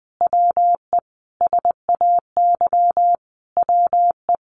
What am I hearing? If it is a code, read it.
Morse code, heard as a keyed tone, about 20 wpm, 700 Hz. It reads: WE SAY WE
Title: Sentences from Top 200 Words 20wpm